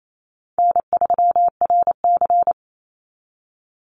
Morse code, heard as a keyed tone, about 28 wpm, 700 Hz. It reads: N3RC
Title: Call Signs Repeated 3x - 28wpm